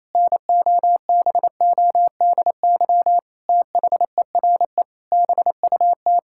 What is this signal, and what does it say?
Morse code, heard as a keyed tone, about 28 wpm, 700 Hz. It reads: NOBODY THERE BUT
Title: Sentences from Top 1600 Words 28wpm